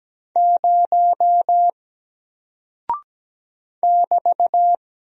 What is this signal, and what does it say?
Morse code, heard as a keyed tone, about 17 wpm, 700 Hz. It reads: 0 <BT>
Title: Single Character (Letter:Number:Punctuation:Prosign) Alt 17wpm